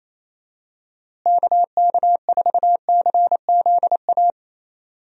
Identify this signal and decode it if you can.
Morse code, heard as a keyed tone, about 28 wpm, 700 Hz. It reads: KK4CZA
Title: USA Call Signs - 6 Characters - Repeated 3x - 28wpm